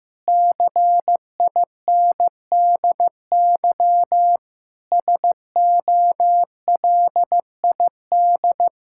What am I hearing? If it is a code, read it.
Morse code, heard as a keyed tone, about 15 wpm, 700 Hz. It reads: CINDY SOLID
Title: K1USN SST QSO Practice 15wpm